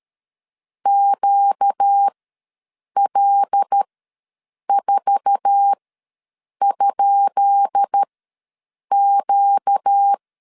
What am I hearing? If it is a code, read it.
Morse code, heard as a keyed tone, about 13 wpm, spaced as for 7 wpm, 800 Hz. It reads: QL4?Q
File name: LEZIONI / 10 SIGNS / 10B INT ws 12wpm 3527